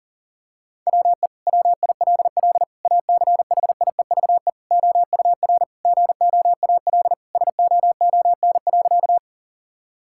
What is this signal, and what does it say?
Morse code, heard as a keyed tone, about 40 wpm, 700 Hz. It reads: WE WILL ACHIEVE OUR GOAL SOON.
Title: Sentences from Top 1000 Words - Review Part 6 - 40wpm